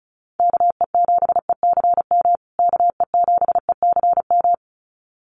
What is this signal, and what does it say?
Morse code, heard as a keyed tone, about 34 wpm, 700 Hz. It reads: KE7ECM KE7ECM
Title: USA Call Signs Repeated 2x 35wpm